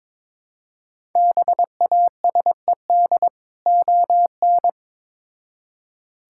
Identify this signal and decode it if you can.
Morse code, heard as a keyed tone, about 22 wpm, 700 Hz. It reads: BASED ON
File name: Build A Phrase - 2-Word Phrases - 22wpm